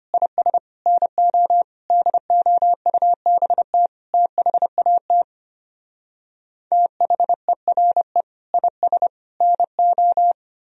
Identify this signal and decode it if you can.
Morse code, heard as a keyed tone, about 33 wpm, 700 Hz. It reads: IS NO DOUBT THAT THERE IS NO